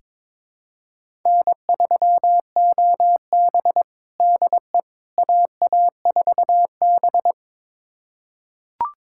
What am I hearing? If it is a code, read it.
Morse code, heard as a keyed tone, about 22 wpm, 700 Hz. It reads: N3OB DE AA4B
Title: USA Callsign DE Callsign - 4 Characters - 22wpm